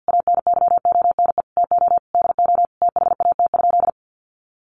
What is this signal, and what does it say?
Morse code, heard as a keyed tone, about 50 wpm, 700 Hz. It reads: ANYONE TO DO THAT?
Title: Sentences from Top 900 Words 50wpm